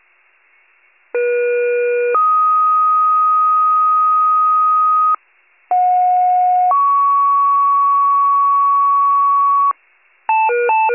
Two-tone page, heard as a one-second tone, then a three-second tone, about 480 then 1250 Hz, about 730 then 1120 Hz. It is heard from 1.1 to 9.7 s.
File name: pager / 39 FIRE / wcalert